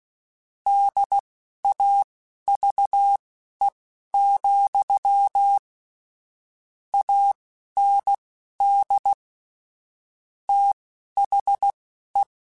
Morse code, heard as a keyed tone, about 16 wpm, spaced as for 11 wpm, 800 Hz. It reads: DAVE, AND THE
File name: QSOs / QSOs-mp3s / kc7haa@10